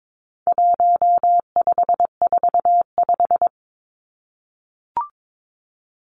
Morse code, heard as a keyed tone, about 22 wpm, 700 Hz. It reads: 1545
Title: Numbers - 4-Digit 22wpm